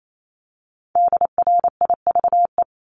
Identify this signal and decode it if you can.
Morse code, heard as a keyed tone, about 28 wpm, 700 Hz. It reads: DRIVE
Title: Multi-letter Phonemes Repeated 3x 28wpm